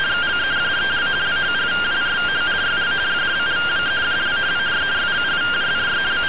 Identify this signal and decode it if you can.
Two-tone teleprinter signal, about 170 Hz shift.
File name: findings / MR CORONAL fr1 -131.gen-21 rtty rx